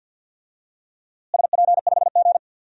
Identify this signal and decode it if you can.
Morse code, heard as a keyed tone, about 52 wpm, 700 Hz. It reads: SP5G
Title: DXCC-100 Call Signs - 4 Characters - Repeated 3x - 50wpm